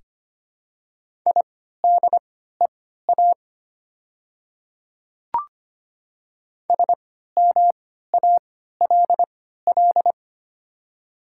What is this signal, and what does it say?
Morse code, heard as a keyed tone, about 25 wpm, spaced as for 14 wpm, 700 Hz. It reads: IDEA SMALL
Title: 3x Character Spacing - Top 500 Words - 25wpm